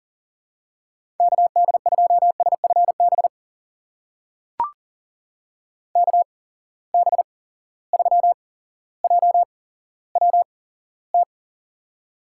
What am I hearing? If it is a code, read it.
Morse code, heard as a keyed tone, about 39 wpm, spaced as for 17 wpm, 700 Hz. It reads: KD2SFB KB3JWT
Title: USA Call Signs - 6 Characters - 8x Character Spacing - 40wpm